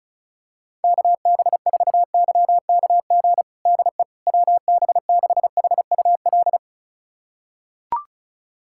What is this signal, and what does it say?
Morse code, heard as a keyed tone, about 36 wpm, 700 Hz. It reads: KB4YKG DE WB6HUL
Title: USA Callsign DE Callsign 35wpm